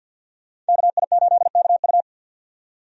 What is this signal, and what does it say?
Morse code, heard as a keyed tone, about 51 wpm, 700 Hz. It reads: KI8KU
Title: USA Call Signs - 5 Characters - 50wpm